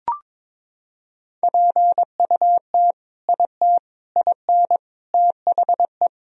Morse code, heard as a keyed tone, about 22 wpm, 700 Hz. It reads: PUT IT IN THE
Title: Sentences from Top 1000 Words - Review Part 2 - 22wpm